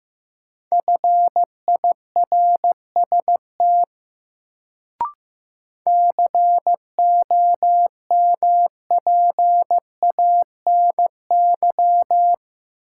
Morse code, heard as a keyed tone, about 15 wpm, 700 Hz. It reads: FIRST COMPANY
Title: Sets of 2 Words - New Word Drill - 15wpm